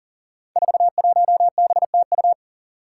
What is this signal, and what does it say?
Morse code, heard as a keyed tone, about 40 wpm, 700 Hz. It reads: 41BTU
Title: Five Challenging Letter-Number Combinations 40wpm